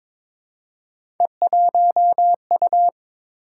Morse code, heard as a keyed tone, about 22 wpm, 700 Hz. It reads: E1U
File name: practice / 22wpm / Three Letter-Number Combinations - 22wpm